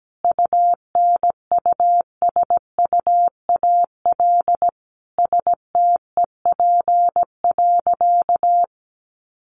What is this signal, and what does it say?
Morse code, heard as a keyed tone, about 17 wpm, 700 Hz. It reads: UNUSUAL STEP.
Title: Sentences from Top 1400 Words 17wpm